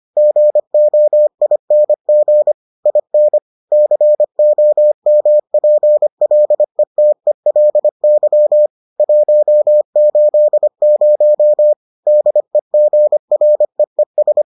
Morse code, heard as a keyed tone, about 25 wpm, 600 Hz. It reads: GOING IN COMPLETELY 180 DEGREES